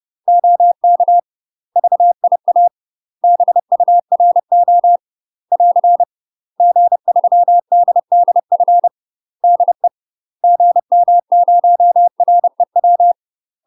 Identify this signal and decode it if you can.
Morse code, heard as a keyed tone, about 30 wpm, 700 Hz. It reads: OK VIA BURO <AR> G3DDF DE GM0REW
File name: UK MASTER QSOs TRAINING / 30WPM / UK MASTER QSOs 0051